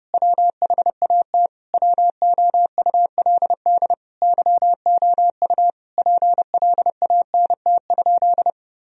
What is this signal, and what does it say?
Morse code, heard as a keyed tone, about 30 wpm, 700 Hz. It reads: WHAT WOULD YOU PLANT?
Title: Sentences from Top 1000 Words - Review Part 1 - 30wpm